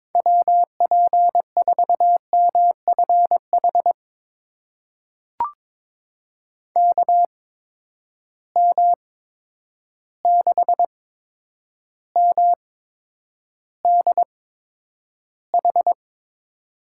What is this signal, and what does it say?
Morse code, heard as a keyed tone, about 22 wpm, spaced as for 9 wpm, 700 Hz. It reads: WP4MFH KM6MDH
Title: USA Call Signs - 6 Characters - 8x Character Spacing - 22wpm